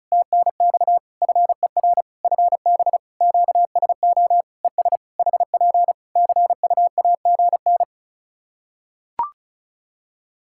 Morse code, heard as a keyed tone, about 36 wpm, 700 Hz. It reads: TNX FER FB QSO ES HP CUAGN
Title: Practice QSO - Exchange 4 - Ending - 35wpm